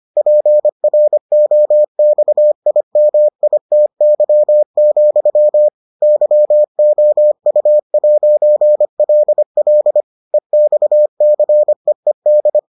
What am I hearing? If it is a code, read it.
Morse code, heard as a keyed tone, about 25 wpm, 600 Hz. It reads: PROXIMITY, YOU'LL EXCEED